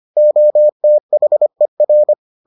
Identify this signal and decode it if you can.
Morse code, heard as a keyed tone, about 25 wpm, 600 Hz. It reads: OTHER